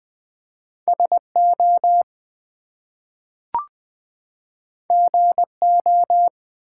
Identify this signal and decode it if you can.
Morse code, heard as a keyed tone, about 20 wpm, 700 Hz. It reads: SO GO
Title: Two Letter Words - 20wpm